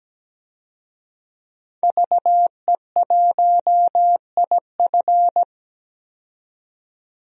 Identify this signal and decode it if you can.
Morse code, heard as a keyed tone, about 17 wpm, 700 Hz. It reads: VE1IF
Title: DXCC-100 Call Signs - 5 Characters - 17wpm